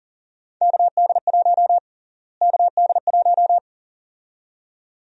Morse code, heard as a keyed tone, about 37 wpm, 700 Hz. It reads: KD1 KD1
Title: USA Call Sign Prefixes - Repeated 2x - 40wpm